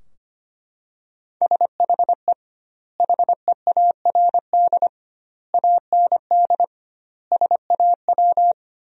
Morse code, heard as a keyed tone, about 25 wpm, 700 Hz. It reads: SHE HEARD AND SAW
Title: 2x Word-Spacing - Sentences from Top 500 Words - 25wpm